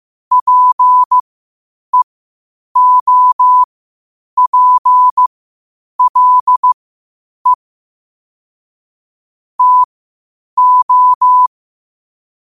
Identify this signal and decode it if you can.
Morse code, heard as a keyed tone, about 15 wpm, spaced as for 8 wpm, 1000 Hz. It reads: PEOPLE TO
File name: qotdfiles / 8wpm